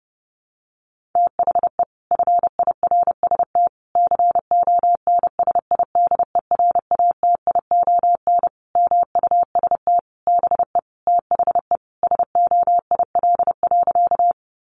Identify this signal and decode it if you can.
Morse code, heard as a keyed tone, about 30 wpm, 700 Hz. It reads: THE FIRST CONSIDERATION MUST BE THE SOIL.
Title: Sentences from Top 1500 Words 30wpm